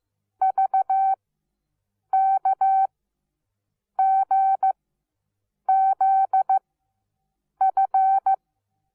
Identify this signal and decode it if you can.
Morse code, heard as a keyed tone, about 15 wpm, spaced as for 7 wpm, 770 Hz. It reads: VKGZF
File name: Onlinelehrgang / morsen / DJ4UF Morsen17